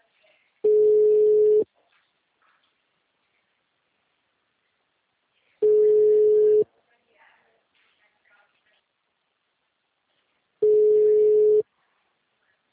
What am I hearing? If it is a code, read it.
Ringback tone (Europe, 425 Hz), with speech.